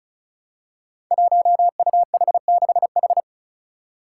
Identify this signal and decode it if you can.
Morse code, heard as a keyed tone, about 36 wpm, 700 Hz. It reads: 1UH6H
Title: Five Challenging Letter-Number Combinations 35wpm